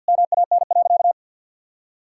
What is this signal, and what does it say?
Morse code, heard as a keyed tone, about 48 wpm, 700 Hz. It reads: MAN.
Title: Sentences from Top 1000 Words - Review Part 5 - 50wpm